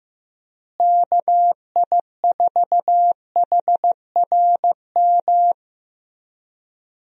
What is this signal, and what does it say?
Morse code, heard as a keyed tone, about 15 wpm, 700 Hz. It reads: KI4HRM
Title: USA Call Signs - 6 Characters - Repeated 3x - 15wpm